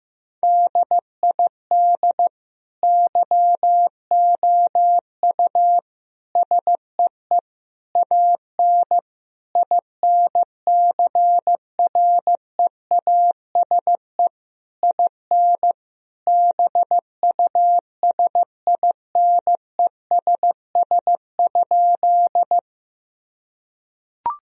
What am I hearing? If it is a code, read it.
Morse code, heard as a keyed tone, about 15 wpm, 700 Hz. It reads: DID YOU SEE AN INCREASE IN BUSINESS?
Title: Sentences from Top 1000 Words - Review Part 4 - 15wpm